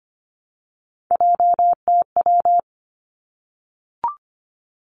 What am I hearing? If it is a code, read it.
Morse code, heard as a keyed tone, about 25 wpm, 700 Hz. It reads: JTW
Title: Three Challenging Letter-Number Combinations 25wpm